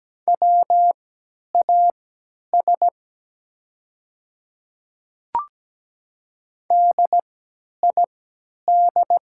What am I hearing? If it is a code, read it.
Morse code, heard as a keyed tone, about 17 wpm, spaced as for 10 wpm, 690 Hz. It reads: WAS DID
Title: 3x Character Spacing - Top 100 Words - 17wpm